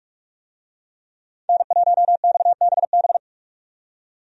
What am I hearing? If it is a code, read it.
Morse code, heard as a keyed tone, about 43 wpm, 700 Hz. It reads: N1XBB
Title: USA Call Signs - 5 Characters - 45wpm